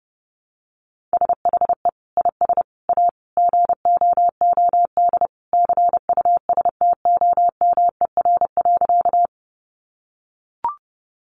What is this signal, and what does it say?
Morse code, heard as a keyed tone, about 30 wpm, 700 Hz. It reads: SHE IS A GOOD CUSTOMER.
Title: Sentences from Top 1100 Words 30wpm